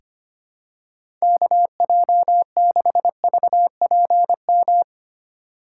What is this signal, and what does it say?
Morse code, heard as a keyed tone, about 25 wpm, 700 Hz. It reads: KJ6VPM
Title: USA Call Signs - 2x Character Spacing - 25wpm